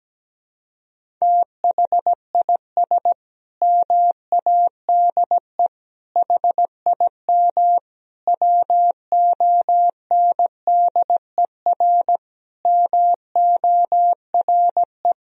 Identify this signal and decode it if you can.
Morse code, heard as a keyed tone, about 17 wpm, 700 Hz. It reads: THIS MADE HIM WONDER MORE
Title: Sentences from Top 500 Words and Abbreviations 17wpm